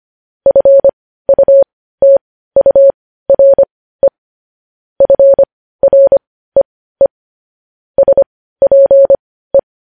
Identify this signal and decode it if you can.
Morse code, heard as a keyed tone, about 25 wpm, spaced as for 15 wpm, 550 Hz. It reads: FUTURE FREE SPE
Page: News Headlines in Morse Code at 15 WPM